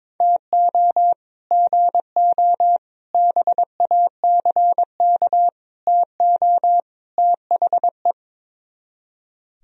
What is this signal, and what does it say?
Morse code, heard as a keyed tone, about 22 wpm, 700 Hz. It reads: TO GO BACK TO THE
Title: Speed-Racing - Sets of 5 Words 22wpm